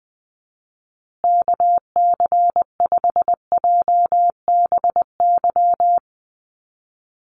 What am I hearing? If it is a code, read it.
Morse code, heard as a keyed tone, about 19 wpm, 700 Hz. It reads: KC5JBY
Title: USA Call Signs Repeated 3x 20wpm